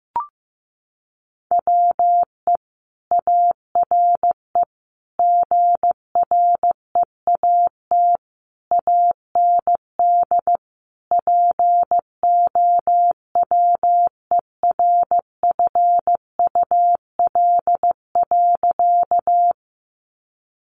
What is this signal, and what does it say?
Morse code, heard as a keyed tone, about 15 wpm, 700 Hz. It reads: WE ARE GREAT AND POWERFUL.
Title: Sentences from Top 1300 Words 15wpm